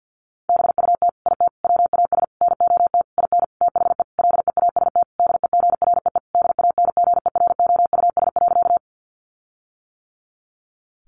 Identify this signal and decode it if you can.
Morse code, heard as a keyed tone, about 50 wpm, 700 Hz. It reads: BUT IT WAS NOT IN THE LEAST DEGREE DANGEROUS.
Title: Sentences from Top 1500 Words 50wpm